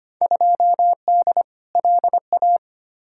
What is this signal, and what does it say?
Morse code, heard as a keyed tone, about 25 wpm, 700 Hz. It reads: 2D LA
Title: Field Day Exchanges - 25wpm